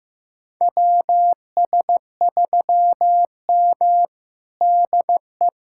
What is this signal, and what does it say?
Morse code, heard as a keyed tone, about 15 wpm, 700 Hz. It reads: WS3M DE
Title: USA Callsign DE Callsign - 4 Characters - 15wpm